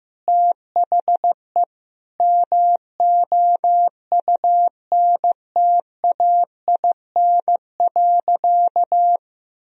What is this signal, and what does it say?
Morse code, heard as a keyed tone, about 15 wpm, 700 Hz. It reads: THE MOUNTAIN.
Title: Sentences from Top 1000 Words - Review Part 1 - 15wpm